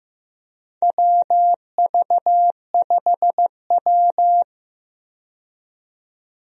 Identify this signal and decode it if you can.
Morse code, heard as a keyed tone, about 15 wpm, 700 Hz. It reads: WV5W
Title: USA Call Signs - 4 Characters - 15wpm